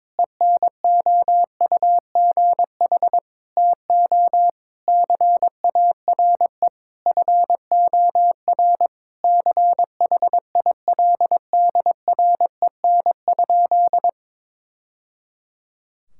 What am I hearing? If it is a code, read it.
Morse code, heard as a keyed tone, about 22 wpm, 700 Hz. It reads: ENOUGH TO CARE FOR CHILDREN?
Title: Sentences from Top 1000 Words - Review Part 5 - 22wpm